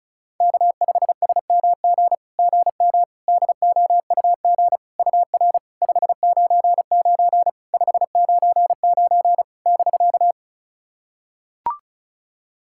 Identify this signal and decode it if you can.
Morse code, heard as a keyed tone, about 36 wpm, 700 Hz. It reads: K5SMG GM DOUG UR 599 599 <BK>
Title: SOTA QSO Practice 35wpm